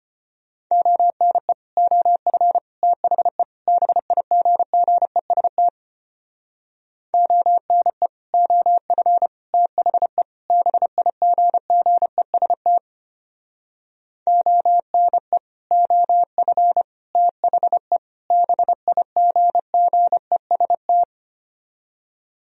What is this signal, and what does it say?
Morse code, heard as a keyed tone, about 31 wpm, 700 Hz. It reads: ONE OF THE BIGGEST ONE OF THE BIGGEST ONE OF THE BIGGEST
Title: Speed-Racing - Sets of 4 Words 25wpm